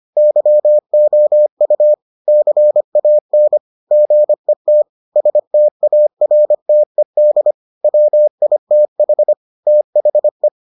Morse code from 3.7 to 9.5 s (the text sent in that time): GET STARTED WITH